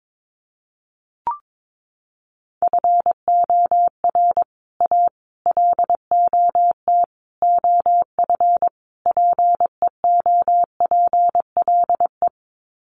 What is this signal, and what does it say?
Morse code, heard as a keyed tone, about 22 wpm, 700 Hz. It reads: FOR A LOT OF PEOPLE
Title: Sets of 5 Words Encore 22wpm